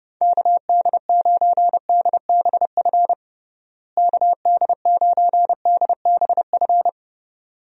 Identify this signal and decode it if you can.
Morse code, heard as a keyed tone, about 30 wpm, 700 Hz. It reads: KD9DBF KD9DBF
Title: USA Call Signs - 6 Characters - Repeated 2x - 30wpm